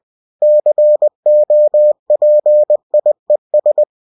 Morse code, heard as a keyed tone, about 20 wpm, 600 Hz. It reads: COPIES
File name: cw-books / winnie-the-pooh / winnie-20wpm-page-0116